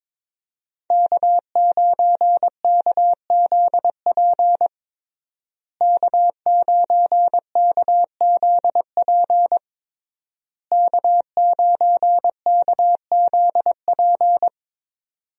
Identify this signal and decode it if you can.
Morse code, heard as a keyed tone, about 22 wpm, 700 Hz. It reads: K9KZP K9KZP K9KZP
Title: USA Call Signs - 5 Characters - Repeated 3x - 22wpm